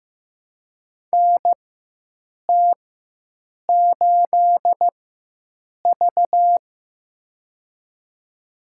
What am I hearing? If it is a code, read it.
Morse code, heard as a keyed tone, about 15 wpm, spaced as for 7 wpm, 700 Hz. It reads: NT8V
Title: USA Call Signs - 4 Characters - 4x Character Spacing - 15wpm